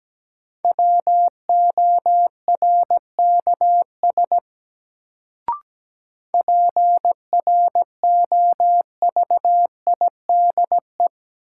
Morse code, heard as a keyed tone, about 17 wpm, 700 Hz. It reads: WORKS PROVIDE
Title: Sentences from Top 1100 Words - New Word Drill Alt - 17wpm